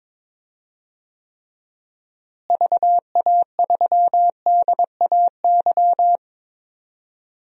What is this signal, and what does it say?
Morse code, heard as a keyed tone, about 21 wpm, 700 Hz. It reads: VA3DAY
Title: DXCC-100 Call Signs - 6 Characters - Repeated 3x - 22wpm